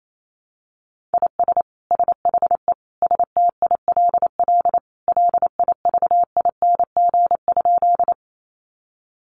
Morse code, heard as a keyed tone, about 28 wpm, 700 Hz. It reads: IS SHE STILL LIVING?
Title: Sentences from Top 1100 Words 28wpm